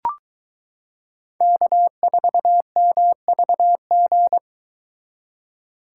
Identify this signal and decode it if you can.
Morse code, heard as a keyed tone, about 22 wpm, 700 Hz. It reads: K4MVG